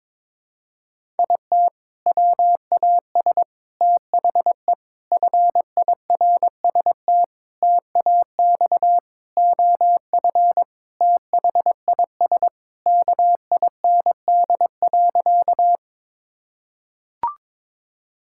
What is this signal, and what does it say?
Morse code, heard as a keyed tone, about 22 wpm, 700 Hz. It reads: IT WAS THE FIRST TAX OF THIS KIND.
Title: Sentences from Top 1000 Words - Review Part 5 - 22wpm